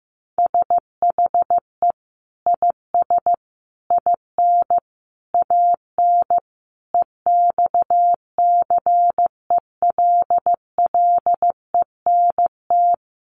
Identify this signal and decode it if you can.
Morse code, heard as a keyed tone, about 15 wpm, 700 Hz. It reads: SHE IS IN AN EXCELLENT
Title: Sentences from Top 1000 Words 15wpm